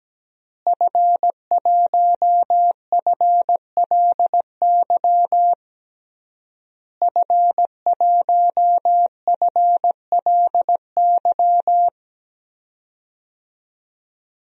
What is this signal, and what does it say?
Morse code, heard as a keyed tone, about 17 wpm, 700 Hz. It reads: F1FLY F1FLY
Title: DXCC-100 Call Signs - 5 Characters - Repeated 3x - 17wpm